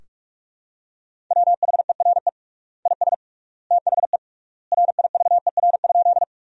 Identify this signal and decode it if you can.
Morse code, heard as a keyed tone, about 42 wpm, 700 Hz. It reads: WHERE IS THE RIVER?
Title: 3x Word-Spacing - Sentences from Top 300 Words - 45wpm